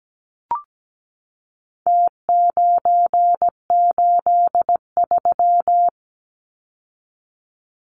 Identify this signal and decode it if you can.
Morse code, heard as a keyed tone, about 17 wpm, 700 Hz. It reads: T983
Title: Numbers - Contest Style - 4-Digit 17wpm